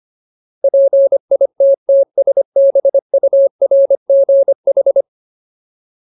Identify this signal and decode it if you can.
Morse code, heard as a keyed tone, about 25 wpm, 550 Hz. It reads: PITTSBURGH
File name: Ultimate Word List / 25WPM Ultimate